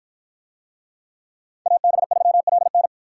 Morse code, heard as a keyed tone, about 53 wpm, 700 Hz. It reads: AB3LN